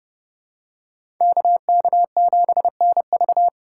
Morse code, heard as a keyed tone, about 28 wpm, 700 Hz. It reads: KK7NV